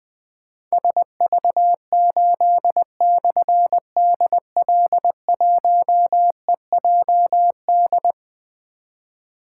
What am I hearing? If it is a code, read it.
Morse code, heard as a keyed tone, about 20 wpm, 700 Hz. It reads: SV8/DL1EJD
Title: Difficult Call Signs 20wpm